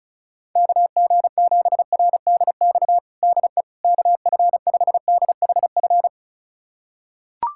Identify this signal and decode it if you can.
Morse code, heard as a keyed tone, about 36 wpm, 700 Hz. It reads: KG7RDX DE KF5DHF
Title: USA Callsign DE Callsign - 6 Characters - 35wpm